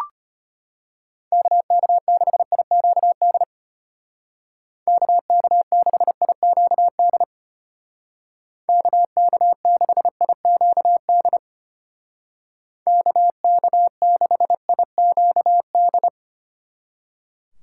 Morse code, heard as a keyed tone, about 32 wpm, 700 Hz. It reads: KK6IQD KK6IQD KK6IQD KK6IQD